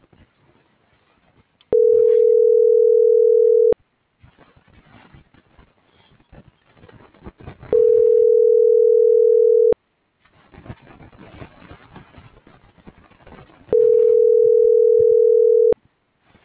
Ringback tone (North America), 440 + 480 Hz.